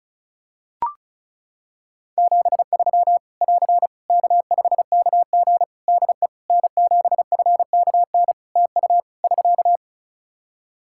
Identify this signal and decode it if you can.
Morse code, heard as a keyed tone, about 36 wpm, 700 Hz. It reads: 73 <AR> K5KG DE N7FKN TU <SK>
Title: K1USN SST QSO Practice 35wpm